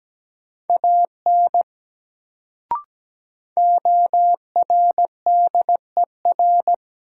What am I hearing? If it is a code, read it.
Morse code, heard as a keyed tone, about 17 wpm, 700 Hz. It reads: AN ORDER
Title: Sets of 2 Words - New Word Drill - 17wpm